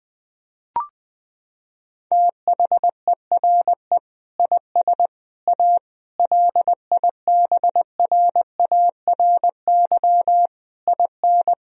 Morse code, heard as a keyed tone, about 21 wpm, 700 Hz. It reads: THERE IS A LIBRARY IN
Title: Sentences from Top 1000 Words - Review Part 4 - 20wpm